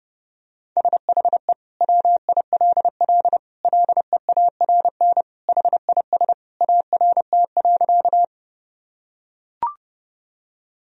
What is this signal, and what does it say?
Morse code, heard as a keyed tone, about 32 wpm, 700 Hz. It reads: SHE WILL LEARN HIS ART.
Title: Sentences from Top 1000 Words - Review Part 3 - 30wpm